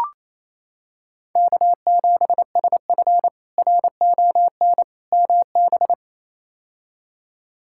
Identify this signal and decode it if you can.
Morse code, heard as a keyed tone, about 28 wpm, 700 Hz. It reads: K7SF RON MB
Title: K1USN SST QSO Practice - 28wpm